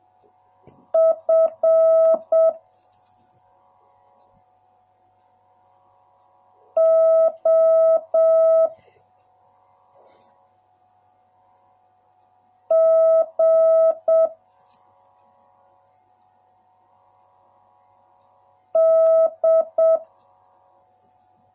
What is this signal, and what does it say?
Morse code, heard as a keyed tone, about 7 wpm, 650 Hz. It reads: F O G D